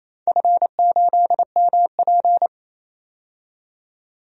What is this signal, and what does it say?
Morse code, heard as a keyed tone, about 28 wpm, 700 Hz. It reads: F8MP
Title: DXCC-100 Call Signs - 4 Characters - 28wpm